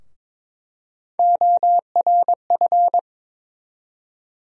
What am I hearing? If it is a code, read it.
Morse code, heard as a keyed tone, about 22 wpm, 700 Hz. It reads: ORF
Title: USA Call Sign Postfixes - 3 Letters 22wpm